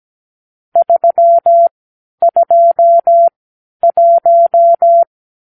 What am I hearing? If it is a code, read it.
Morse code, heard as a keyed tone, about 17 wpm, spaced as for 10 wpm, 680 Hz. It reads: 321
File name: k7qo / Lesson 46 (1-4)